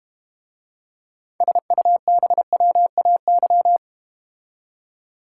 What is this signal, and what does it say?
Morse code, heard as a keyed tone, about 31 wpm, 700 Hz. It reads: SUBWAY
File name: Easy Words - Speed-Racing with Hints - 25wpm